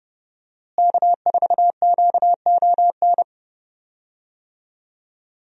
Morse code, heard as a keyed tone, about 30 wpm, 700 Hz. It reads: K4QON
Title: USA Call Signs - 5 Characters - Repeated 2x - 30wpm